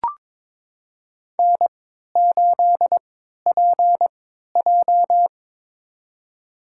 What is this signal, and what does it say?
Morse code, heard as a keyed tone, about 22 wpm, spaced as for 13 wpm, 700 Hz. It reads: N8PJ
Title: USA Call Signs - 4 Characters - 3x Character Spacing - 22wpm